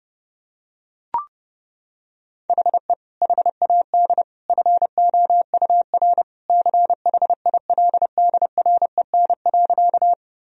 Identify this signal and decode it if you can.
Morse code, heard as a keyed tone, about 32 wpm, 700 Hz. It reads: HE HAD FOUR CHILDREN.
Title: Sentences from Top 1000 Words - Review Part 1 - 30wpm